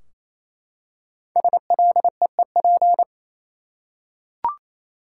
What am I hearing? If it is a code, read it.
Morse code, heard as a keyed tone, about 28 wpm, 700 Hz. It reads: SLEEP
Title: Five Letter Words - 28wpm